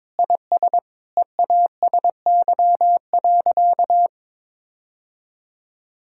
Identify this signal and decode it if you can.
Morse code, heard as a keyed tone, about 22 wpm, 700 Hz. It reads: IS EASY.
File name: Sentences with 7-Word Phrases - 22wpm